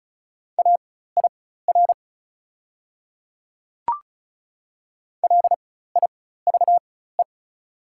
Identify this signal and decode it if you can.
Morse code, heard as a keyed tone, about 34 wpm, spaced as for 16 wpm, 690 Hz. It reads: AIR LIVE
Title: 4x Character Spacing - Top 200 Words - 35wpm